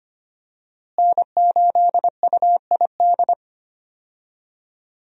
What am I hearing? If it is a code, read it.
Morse code, heard as a keyed tone, about 25 wpm, 700 Hz. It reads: N8UID
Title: USA Call Signs - 5 Characters - Repeated 2x - 25wpm